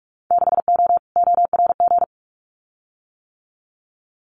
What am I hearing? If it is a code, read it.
Morse code, heard as a keyed tone, about 46 wpm, 700 Hz. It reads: 6O ORG
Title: Winter Field Day Exchanges Encore 45wpm